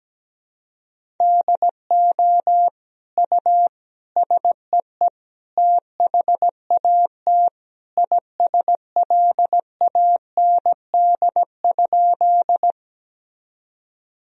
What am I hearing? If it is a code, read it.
Morse code, heard as a keyed tone, about 17 wpm, 700 Hz. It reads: DO U SEE THAT ISLAND?
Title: Sentences from Top 500 Words and Abbreviations 17wpm